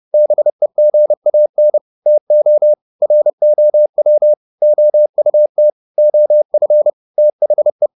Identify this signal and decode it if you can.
Morse code, heard as a keyed tone, about 30 wpm, 600 Hz. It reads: BEGAN TO ROW OUT OF THE
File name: the old man and the sea / the old man and the sea 1 30wpm